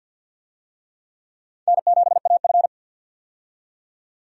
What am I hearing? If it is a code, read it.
Morse code, heard as a keyed tone, about 51 wpm, 700 Hz. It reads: N7AF
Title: USA Call Signs - 4 Characters - 50wpm